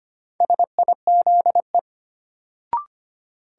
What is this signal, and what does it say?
Morse code, heard as a keyed tone, about 25 wpm, 700 Hz. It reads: SIZE
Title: Sentences from Top 400 Words - New Word Drill - 25wpm